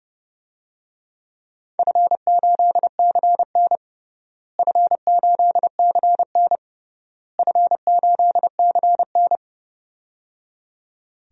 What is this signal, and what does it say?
Morse code, heard as a keyed tone, about 30 wpm, 700 Hz. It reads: F8CN F8CN F8CN
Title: DXCC-100 Call Signs - 4 Characters - Repeated 3x - 30wpm